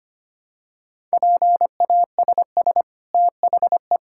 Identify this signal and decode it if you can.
Morse code, heard as a keyed tone, about 25 wpm, 700 Hz. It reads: PASS THE
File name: Easy Sentences with Hints - 25wpm